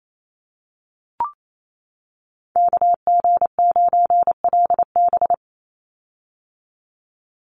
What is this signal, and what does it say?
Morse code, heard as a keyed tone, about 28 wpm, 700 Hz. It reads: KG9LB
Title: Call Signs - 28wpm